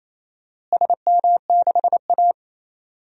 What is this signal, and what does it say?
Morse code, heard as a keyed tone, about 28 wpm, 700 Hz. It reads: SM6A
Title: DXCC-100 Call Signs - 4 Characters - Repeated 2x - 28wpm